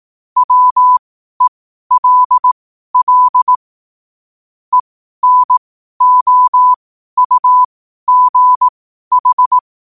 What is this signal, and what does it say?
Morse code, heard as a keyed tone, about 18 wpm, spaced as for 12 wpm, 1000 Hz. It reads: WELL ENOUGH
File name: 12wpm